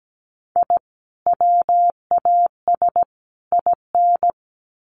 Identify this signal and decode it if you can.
Morse code, heard as a keyed tone, about 17 wpm, 700 Hz. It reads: I WAS IN
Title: Sets of 3 Words 17wpm